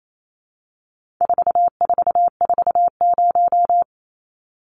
Morse code, heard as a keyed tone, about 28 wpm, 700 Hz. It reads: 4440
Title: Numbers - 4-Digit 28wpm